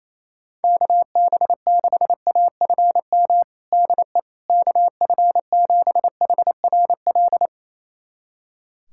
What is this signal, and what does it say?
Morse code, heard as a keyed tone, about 28 wpm, 700 Hz. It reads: KB6AFM DE KF7HRL
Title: USA Callsign DE Callsign - 6 Characters - 28wpm